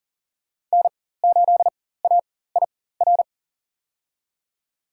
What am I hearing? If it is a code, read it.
Morse code, heard as a keyed tone, about 37 wpm, spaced as for 18 wpm, 700 Hz. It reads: N8AIR
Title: USA Call Signs - 5 Characters - 4x Character Spacing - 40wpm